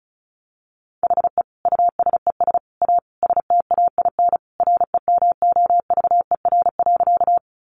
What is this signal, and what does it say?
Morse code, heard as a keyed tone, about 34 wpm, 700 Hz. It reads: HE USES A STAIN REMOVER.
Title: Additional Easy Phrases 35wpm